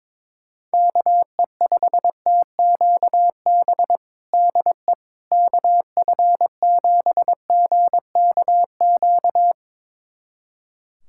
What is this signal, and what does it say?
Morse code, heard as a keyed tone, about 22 wpm, 700 Hz. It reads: KE5TQB DE KF7GKQ
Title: USA Callsign DE Callsign - 6 Characters - 22wpm